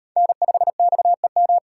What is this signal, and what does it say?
Morse code, heard as a keyed tone, about 38 wpm, 700 Hz. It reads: N5XEM